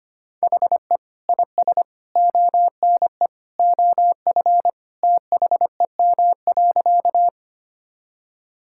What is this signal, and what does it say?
Morse code, heard as a keyed tone, about 25 wpm, 700 Hz. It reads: HE IS ONE OF THEM.
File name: practice / 25wpm / Sentences with 3-Word Phrases - 25wpm